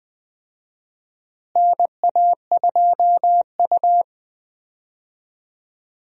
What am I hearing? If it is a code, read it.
Morse code, heard as a keyed tone, about 21 wpm, 700 Hz. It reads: NA2U
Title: USA Call Signs - 4 Characters - Repeated 2x - 20wpm